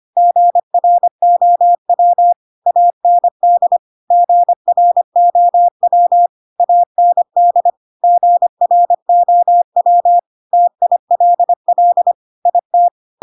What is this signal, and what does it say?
Morse code, heard as a keyed tone, about 25 wpm, 700 Hz. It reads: GROW AND GROW AND GROW TILL IT
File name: Pinocchio / Pinocchio ch3 25 25 ik1hge